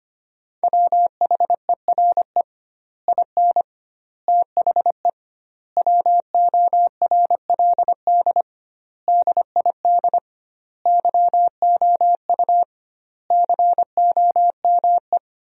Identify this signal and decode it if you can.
Morse code, heard as a keyed tone, about 25 wpm, 700 Hz. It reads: WHERE IN THE WORLD DID YOU COME
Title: 2x Word-Spacing - Sentences from Top 200 Words - 25wpm